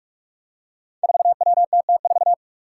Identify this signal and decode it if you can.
Morse code, heard as a keyed tone, about 44 wpm, 700 Hz. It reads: 4WTT4
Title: Five Challenging Letter-Number Combinations 45wpm